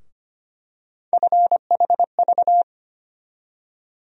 Morse code, heard as a keyed tone, about 25 wpm, 700 Hz. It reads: FHV
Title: USA Call Sign Postfixes - 3 Letters 25wpm